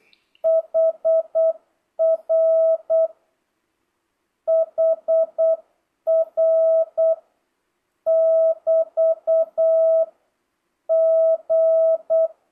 Morse code, heard as a keyed tone, about 8 wpm, 650 Hz. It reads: HR HR <BT> G
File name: jh8cbh / 2025cw